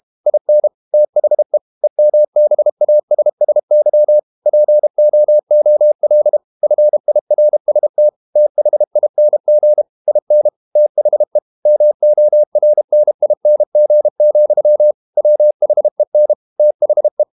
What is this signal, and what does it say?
Morse code, heard as a keyed tone, about 32 wpm, 600 Hz. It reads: IN THE EMBASSY POOL FIRST THING IN THE MORNING, WHEN THE
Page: Never 31wpm | CW med Gnister